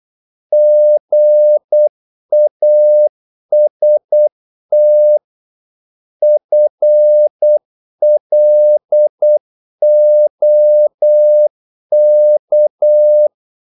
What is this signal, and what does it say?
Morse code, heard as a keyed tone, about 8 wpm, 600 Hz. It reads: GAST FLOK
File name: Korte ord DK 8wpm